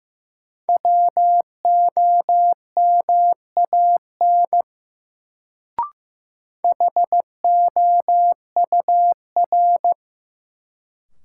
Morse code, heard as a keyed tone, about 15 wpm, 700 Hz. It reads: WOMAN HOUR
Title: Sets of 5 Words - New Word Drill - 15wpm